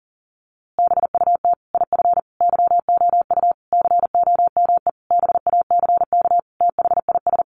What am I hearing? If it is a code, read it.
Morse code, heard as a keyed tone, about 40 wpm, 700 Hz. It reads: BUT IF YOU COME BACK THIS
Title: A Sound Of Thunder 40wpm